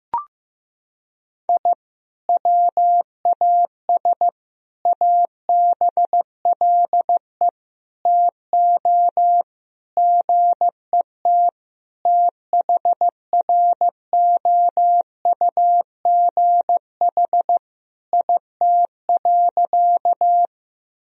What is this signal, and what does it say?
Morse code, heard as a keyed tone, about 15 wpm, 700 Hz. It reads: I WAS ABLE TO GET THROUGH IT.
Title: Sentences from Top 1000 Words - Review Part 2 - 15wpm